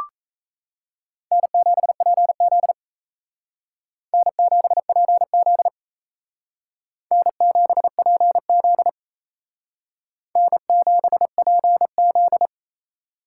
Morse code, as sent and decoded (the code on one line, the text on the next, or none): -. --... .--. --.. / -. --... .--. --.. / -. --... .--. --.. / -. --... .--. --..
N7PZ N7PZ N7PZ N7PZ